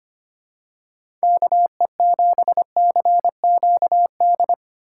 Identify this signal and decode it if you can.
Morse code, heard as a keyed tone, about 25 wpm, 700 Hz. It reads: KE7CQD
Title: USA Call Signs - 6x Character Spacing - 25wpm